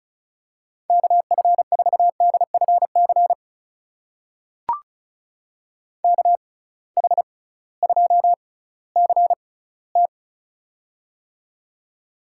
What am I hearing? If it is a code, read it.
Morse code, heard as a keyed tone, about 36 wpm, spaced as for 19 wpm, 700 Hz. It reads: KF4DFC KH2CT
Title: USA Call Signs - 6x Character Spacing - 35wpm